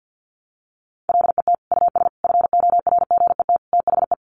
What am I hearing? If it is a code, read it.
Morse code, heard as a keyed tone, about 50 wpm, 700 Hz. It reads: LET US FORGET THE
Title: Sentences from Top 1200 Words 50wpm